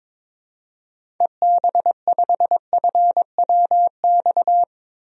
Morse code, heard as a keyed tone, about 22 wpm, 700 Hz. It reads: EB5FWX
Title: DXCC-100 Call Signs - 6 Characters - 22wpm